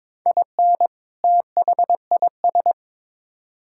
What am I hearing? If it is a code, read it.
Morse code, heard as a keyed tone, about 22 wpm, 700 Hz. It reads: IN THIS
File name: practice / 22wpm / Build A Phrase - 5-Word Phrases - 22wpm